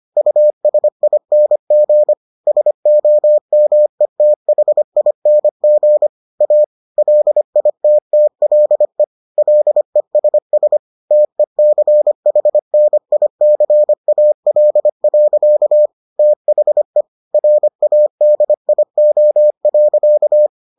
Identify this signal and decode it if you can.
Morse code, heard as a keyed tone, about 25 wpm, 600 Hz. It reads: USING SOMETHING A LITTLE LESS TECHNICAL. THE RADIO.